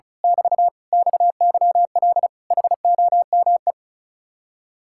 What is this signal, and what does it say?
Morse code, heard as a keyed tone, about 33 wpm, 700 Hz. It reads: <BT> XYL HOME
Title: Speed-Racing Common QSO Phrases - 35wpm